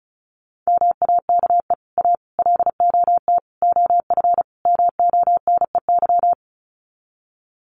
Morse code, heard as a keyed tone, about 34 wpm, 700 Hz. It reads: MAKE A LOT OF MONEY
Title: Sets of 5 Words Encore2 35wpm